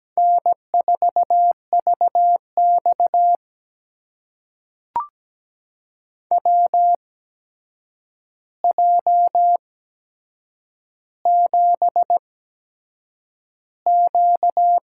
Morse code, heard as a keyed tone, about 17 wpm, spaced as for 7 wpm, 700 Hz. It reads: N4VX WJ7Q
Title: USA Call Signs - 4 Characters - 8x Character Spacing - 17wpm